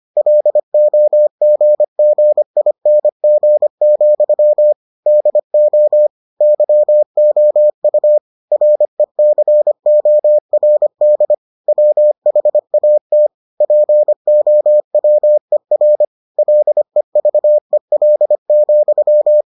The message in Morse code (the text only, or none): LOGGING, DO YOU RECORD WHAT POWER LEVEL,